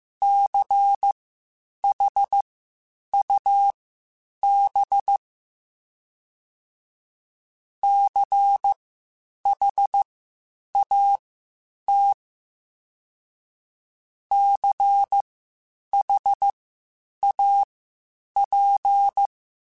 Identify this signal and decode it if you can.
Morse code, heard as a keyed tone, about 15 wpm, spaced as for 8 wpm, 800 Hz. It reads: CHUB CHAT CHAP
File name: Spaced Alphas / Spaced Alphas mp3s / Cs sp @13